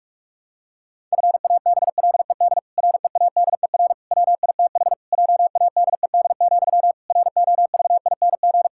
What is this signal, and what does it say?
Morse code, heard as a keyed tone, about 48 wpm, 700 Hz. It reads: FABLED READER WITH JADED, ROVING
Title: Pangram Challenge v2 - 45wpm